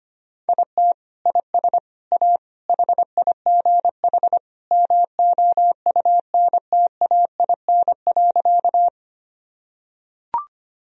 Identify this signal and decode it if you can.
Morse code, heard as a keyed tone, about 25 wpm, 700 Hz. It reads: IT IS A HIGH MOUNTAIN.
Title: Sentences from Top 1000 Words - Review Part 1 - 25wpm